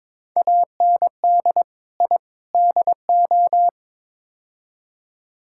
Morse code, heard as a keyed tone, about 22 wpm, 700 Hz. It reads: AND I DO
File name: practice / 22wpm / Build A Phrase - 5-Word Phrases - 22wpm